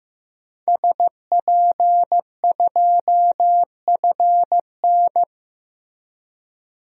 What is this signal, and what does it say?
Morse code, heard as a keyed tone, about 15 wpm, 700 Hz. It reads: SP2FN
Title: DXCC-100 Call Signs - 5 Characters - Repeated 2x - 15wpm